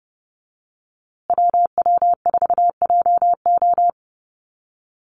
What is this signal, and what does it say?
Morse code, heard as a keyed tone, about 30 wpm, 700 Hz. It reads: WW4JO
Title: USA Call Signs Repeated 2x 30wpm